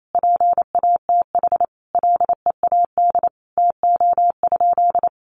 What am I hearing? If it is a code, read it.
Morse code, heard as a keyed tone, about 28 wpm, 700 Hz. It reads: PATH LEAD TO?
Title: Sentences from Top 1400 Words 28wpm